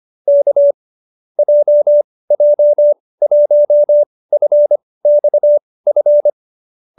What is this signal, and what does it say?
Morse code, heard as a keyed tone, about 25 wpm, spaced as for 18 wpm, 570 Hz. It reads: K JJ1FXF